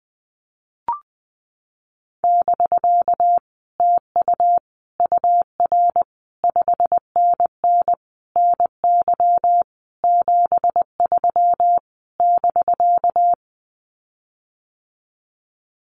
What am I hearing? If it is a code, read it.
Morse code, heard as a keyed tone, about 19 wpm, 700 Hz. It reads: <BK> TU UR 5NN NY 73 <BK>
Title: Common QSO Phrases Encore 20wpm